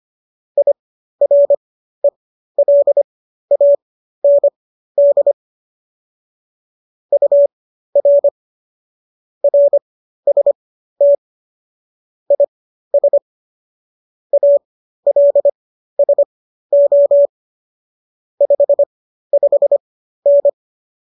Morse code, heard as a keyed tone, about 25 wpm, spaced as for 13 wpm, 570 Hz. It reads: IRELAND UR RST IS ALSO 55N